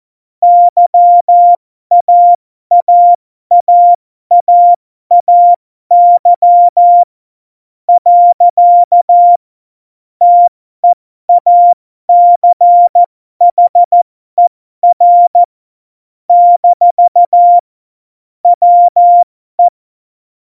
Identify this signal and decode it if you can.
Morse code, heard as a keyed tone, about 14 wpm, spaced as for 12 wpm, 700 Hz. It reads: YAAAAAY . TEACHER - WE
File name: Morsekurs / Witze